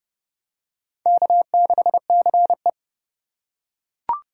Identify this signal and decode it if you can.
Morse code, heard as a keyed tone, about 29 wpm, 700 Hz. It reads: K6CE